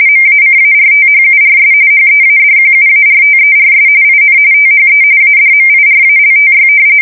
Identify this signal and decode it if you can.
RTTY, about 45.45 baud. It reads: MACARTHURS HEADQUARTERS IN THE ADMIRALTY
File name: fsk170 / battleshipnj